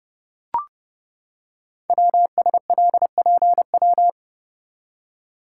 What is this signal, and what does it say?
Morse code, heard as a keyed tone, about 28 wpm, 690 Hz. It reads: WSLPW
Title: Code Groups - 5 Letters - 30wpm